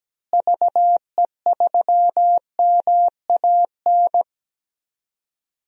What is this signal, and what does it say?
Morse code, heard as a keyed tone, about 17 wpm, 700 Hz. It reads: VE3MAN
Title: DXCC-100 Call Signs - 6 Characters - Repeated 2x - 17wpm